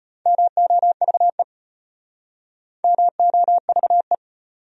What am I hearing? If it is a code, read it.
Morse code, heard as a keyed tone, about 33 wpm, 700 Hz. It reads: MOVE MOVE